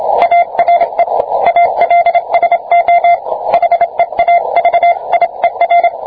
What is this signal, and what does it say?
Morse code, heard as a keyed tone, about 27 wpm, 690 Hz. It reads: ARE ALSO HEAVIER